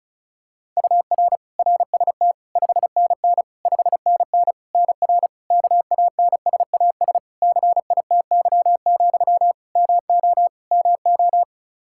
Morse code, heard as a keyed tone, about 36 wpm, 700 Hz. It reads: UR RST 5NN 5NN NR KANSAS CITY, MO MO
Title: Practice QSO - Exchange 1 - Beginning - 35wpm